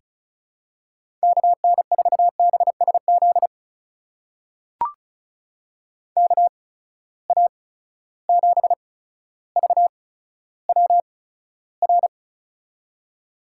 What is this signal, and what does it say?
Morse code, heard as a keyed tone, about 36 wpm, spaced as for 15 wpm, 700 Hz. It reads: KN4BSZ KA7VWR
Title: USA Call Signs - 6 Characters - 8x Character Spacing - 35wpm